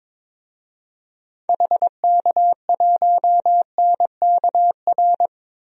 Morse code, heard as a keyed tone, about 22 wpm, 700 Hz. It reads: HK1NKR
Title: DXCC-100 Call Signs - 6 Characters - Repeated 2x - 22wpm